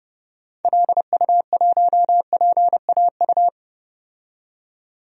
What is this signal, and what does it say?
Morse code, heard as a keyed tone, about 32 wpm, 700 Hz. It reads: LU1PAU
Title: DXCC-100 Call Signs - 6 Characters - 30wpm